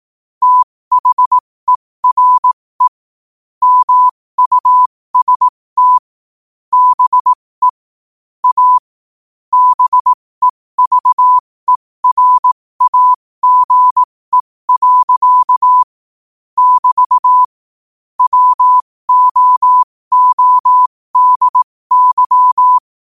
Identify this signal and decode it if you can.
Morse code, heard as a keyed tone, about 18 wpm, spaced as for 15 wpm, 1000 Hz. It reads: THERE MUST BE A BEVERAGE. <BT> WOODY
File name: qotdfiles / 15wpm